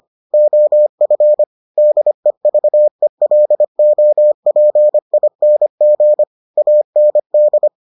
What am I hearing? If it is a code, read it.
Morse code, heard as a keyed tone, about 25 wpm, 600 Hz. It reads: OF DEVELOPING AND